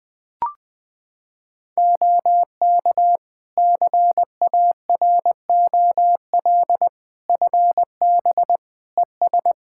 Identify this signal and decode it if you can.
Morse code, heard as a keyed tone, about 20 wpm, 700 Hz. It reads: OK CAROL FB ES
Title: K1USN SST QSO Practice 20wpm